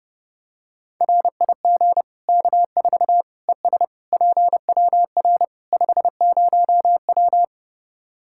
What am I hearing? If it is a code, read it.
Morse code, heard as a keyed tone, about 32 wpm, 700 Hz. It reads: RIG K4 ES PWR 50W
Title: Practice QSO - Exchange 2 - Info - 30wpm